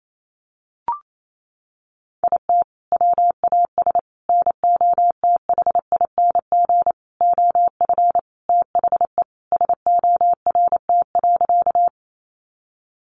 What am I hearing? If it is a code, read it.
Morse code, heard as a keyed tone, about 28 wpm, 700 Hz. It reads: IT WAS NOTHING OF THE SORT.
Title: Sentences from Top 1300 Words 28wpm